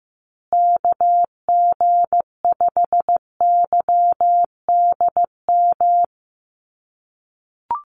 Morse code, heard as a keyed tone, about 15 wpm, 700 Hz. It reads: KG5YDM
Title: USA Call Signs Repeated 3x 15wpm